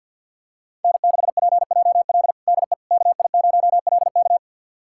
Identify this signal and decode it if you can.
Morse code, heard as a keyed tone, about 48 wpm, 700 Hz. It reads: N6PJL DE KI0LK
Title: USA Callsign DE Callsign - 5 Characters - 50wpm